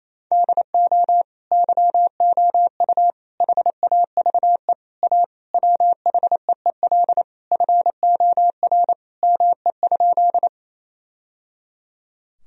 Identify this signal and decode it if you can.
Morse code, heard as a keyed tone, about 28 wpm, 700 Hz. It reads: DO YOU HAVE A WHEEL FOR ME?
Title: Sentences from Top 1000 Words - Review Part 3 - 28wpm